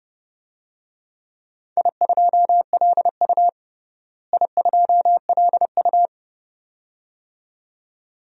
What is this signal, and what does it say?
Morse code, heard as a keyed tone, about 30 wpm, 700 Hz. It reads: I2LU I2LU
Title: DXCC-100 Call Signs - 4 Characters - Repeated 2x - 30wpm